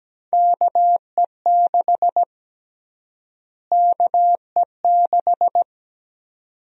Morse code, heard as a keyed tone, about 17 wpm, 700 Hz. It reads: KE6 KE6
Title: USA Call Sign Prefixes - Repeated 2x - 17wpm